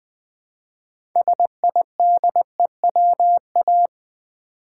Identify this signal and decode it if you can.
Morse code, heard as a keyed tone, about 20 wpm, 700 Hz. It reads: SIDEWA
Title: Build A Word - 2 Letters At A Time - 8-Letter Words - 20wpm